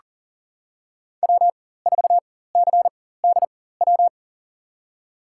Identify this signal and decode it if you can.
Morse code, heard as a keyed tone, about 40 wpm, spaced as for 19 wpm, 700 Hz. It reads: W4CDW
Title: USA Call Signs - 5 Characters - 4x Character Spacing - 40wpm